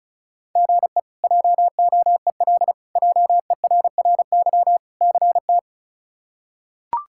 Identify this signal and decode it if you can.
Morse code, heard as a keyed tone, about 36 wpm, 700 Hz. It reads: GE JOEL JERRY CT
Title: K1USN SST QSO Practice - 35wpm